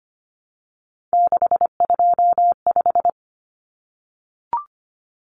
Morse code, heard as a keyed tone, about 25 wpm, 700 Hz. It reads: 625
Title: Numbers - 3-Digit 25wpm